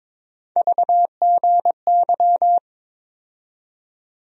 Morse code, heard as a keyed tone, about 22 wpm, 700 Hz. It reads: VGY
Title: USA Call Sign Postfixes - 3 Letters 22wpm